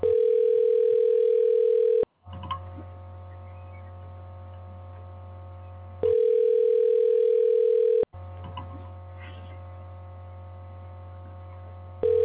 Ringback tone (North America), 440 + 480 Hz.
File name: recordings / RE361f7e0016e2e882e f8d7b8c